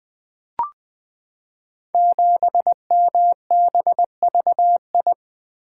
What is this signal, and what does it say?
Morse code, heard as a keyed tone, about 20 wpm, 700 Hz. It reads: 7MBVI
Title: Code Groups - 5 Letter:Numbers - 20wpm